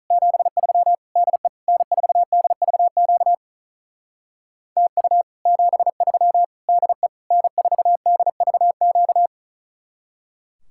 Morse code, heard as a keyed tone, about 45 wpm, 700 Hz. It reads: 73 DE N4DVQ TU 73 DE N4DVQ